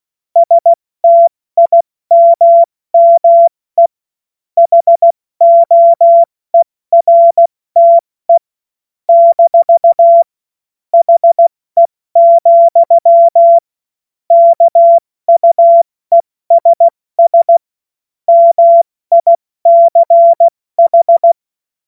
Morse code, heard as a keyed tone, about 16 wpm, 680 Hz. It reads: STIMME HOERTE - HE, KUESS MICH
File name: Morsekurs / Witze / Witz 08. November 2022 70 F80